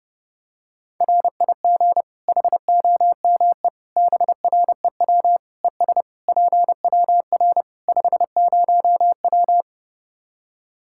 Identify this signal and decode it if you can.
Morse code, heard as a keyed tone, about 30 wpm, 700 Hz. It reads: RIG HOME BREW ES PWR 50W
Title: Practice QSO - Exchange 2 - Info - 30wpm